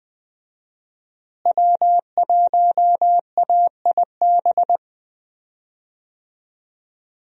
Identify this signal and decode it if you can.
Morse code, heard as a keyed tone, about 20 wpm, 700 Hz. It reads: W1AIB
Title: USA Call Signs - 5 Characters - Repeated 2x - 20wpm